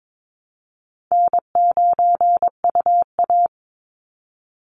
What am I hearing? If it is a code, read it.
Morse code, heard as a keyed tone, about 22 wpm, 700 Hz. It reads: N9UA
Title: Easy Call Signs Repeated 2x - 22wpm